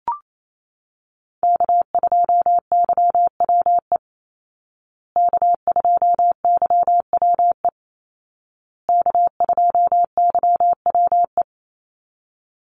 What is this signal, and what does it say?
Morse code, heard as a keyed tone, about 28 wpm, 700 Hz. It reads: K2YWE K2YWE K2YWE
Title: Call Signs Repeated 2x - 28wpm